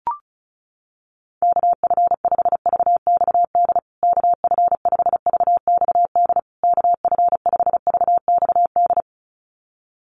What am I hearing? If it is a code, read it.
Morse code, heard as a keyed tone, about 34 wpm, 700 Hz. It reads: KF5VXD KF5VXD KF5VXD
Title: USA Call Signs Repeated 3x 35wpm